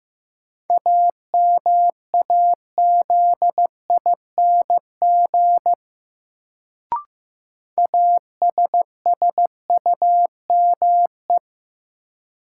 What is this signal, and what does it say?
Morse code, heard as a keyed tone, about 15 wpm, 700 Hz. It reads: AMAZING ASSUME
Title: Sentences from Top 1500 Words - New Word Drill - 15wpm